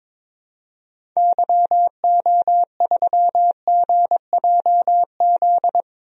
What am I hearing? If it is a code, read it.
Morse code, heard as a keyed tone, about 22 wpm, 700 Hz. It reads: YO3GJZ
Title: DXCC-100 Call Signs - 6 Characters - 22wpm